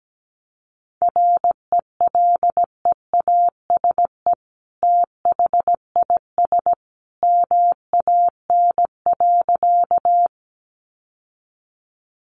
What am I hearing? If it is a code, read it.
Morse code, heard as a keyed tone, about 17 wpm, 700 Hz. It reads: RELEASE THIS MAN.
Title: Sentences from Top 1100 Words 17wpm